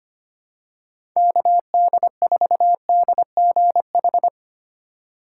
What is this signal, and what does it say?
Morse code, heard as a keyed tone, about 25 wpm, 700 Hz. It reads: KD4DGH
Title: USA Call Signs - 6 Characters - 2x Character Spacing - 25wpm